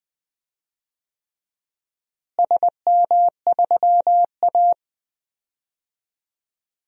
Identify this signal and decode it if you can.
Morse code, heard as a keyed tone, about 21 wpm, 700 Hz. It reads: SM3A
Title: DXCC-100 Call Signs - 4 Characters - Repeated 3x - 20wpm